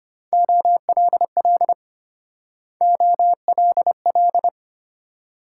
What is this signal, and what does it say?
Morse code, heard as a keyed tone, about 29 wpm, 700 Hz. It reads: OLL OLL
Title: Speed-Racing Multi-letter Phonemes - 25wpm